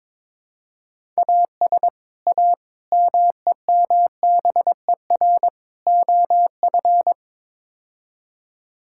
Morse code, heard as a keyed tone, about 22 wpm, 700 Hz. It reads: AS A MEMBER OF
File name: practice / 22wpm / Build A Phrase - 5-Word Phrases - 22wpm